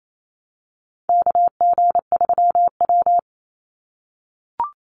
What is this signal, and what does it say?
Morse code, heard as a keyed tone, about 28 wpm, 700 Hz. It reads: KG3W
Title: NA SOTA Chaser Callsigns 28wpm